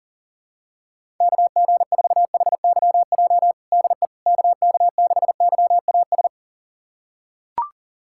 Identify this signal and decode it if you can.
Morse code, heard as a keyed tone, about 40 wpm, 700 Hz. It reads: KG4HYJ DE KK6YAS
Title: USA Callsign DE Callsign - 6 Characters - 40wpm